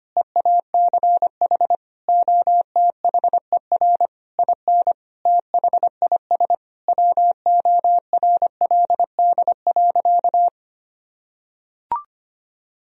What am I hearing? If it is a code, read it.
Morse code, heard as a keyed tone, about 25 wpm, 700 Hz. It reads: EACH OTHER IN THIS WORLD.
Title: Sentences from Top 1000 Words - Review Part 1 - 25wpm